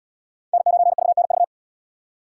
Morse code, heard as a keyed tone, about 67 wpm, 700 Hz. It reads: N9VNV